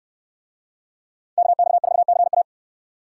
Morse code, heard as a keyed tone, about 65 wpm, 700 Hz. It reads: KX4CA